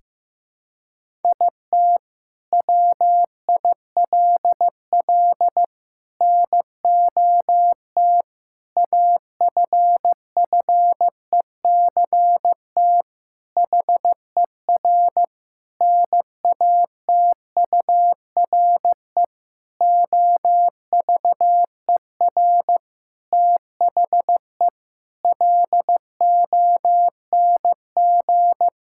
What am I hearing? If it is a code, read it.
Morse code, heard as a keyed tone, about 15 wpm, 700 Hz. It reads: IT WILL NOT AFFECT HER NATURE OVER THE LONG
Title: Sentences from Top 1000 Words - Review Part 6 - 15wpm